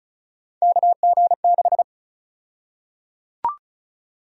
Morse code, heard as a keyed tone, about 34 wpm, 700 Hz. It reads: KG6
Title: USA Call Sign Prefixes - 35wpm